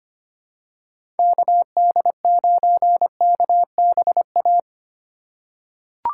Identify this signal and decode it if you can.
Morse code, heard as a keyed tone, about 25 wpm, 700 Hz. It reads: KD9KBA
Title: USA Call Signs - 4x Character Spacing - 25wpm